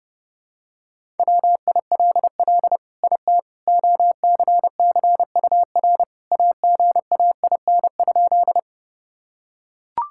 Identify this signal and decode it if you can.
Morse code, heard as a keyed tone, about 30 wpm, 700 Hz. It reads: WILL IT OCCUR AGAIN?
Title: Sentences from Top 1000 Words - Review Part 6 - 30wpm